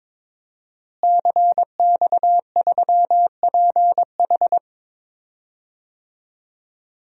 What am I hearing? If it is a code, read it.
Morse code, heard as a keyed tone, about 22 wpm, 700 Hz. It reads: CX3PH
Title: DXCC-100 Call Signs - 5 Characters - Repeated 3x - 22wpm